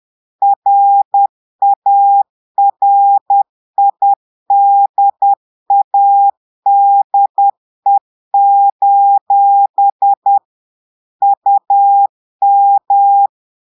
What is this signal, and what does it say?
Morse code, heard as a keyed tone, about 10 wpm, 800 Hz. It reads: RARIDADE: UM
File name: Portugais - texte en clair 1 - vitesse - 10 mots minute